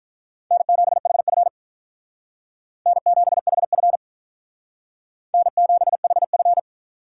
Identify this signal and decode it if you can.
Morse code, heard as a keyed tone, about 45 wpm, 700 Hz. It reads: N7HF N7HF N7HF